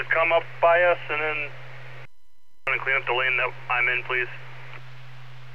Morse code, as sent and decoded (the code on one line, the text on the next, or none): none